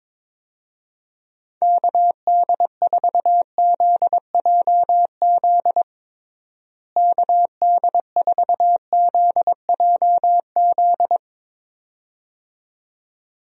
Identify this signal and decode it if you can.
Morse code, heard as a keyed tone, about 21 wpm, 700 Hz. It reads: KD4ZJZ KD4ZJZ
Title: USA Call Signs - 6 Characters - Repeated 2x - 22wpm